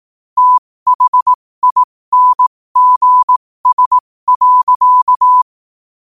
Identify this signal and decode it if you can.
Morse code, heard as a keyed tone, about 18 wpm, spaced as for 15 wpm, 1000 Hz. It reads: THINGS.
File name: qotdfiles / 15wpm